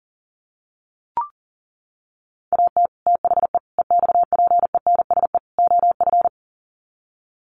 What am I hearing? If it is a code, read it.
Morse code, heard as a keyed tone, about 40 wpm, 700 Hz. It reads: AT THE EXPENSE OF
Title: Sets of 4 Words 40wpm